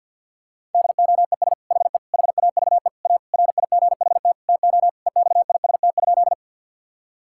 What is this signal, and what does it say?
Morse code, heard as a keyed tone, about 50 wpm, 700 Hz. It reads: DOES HE HAVE A RIGHT TO EXIST?
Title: Sentences from Top 1000 Words - Review Part 5 - 50wpm